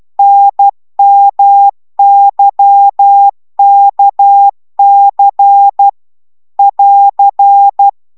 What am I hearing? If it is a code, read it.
Morse code, heard as a keyed tone, about 12 wpm, 800 Hz. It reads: NMYKC <AR>
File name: taviro tanulas / cw 69 19 10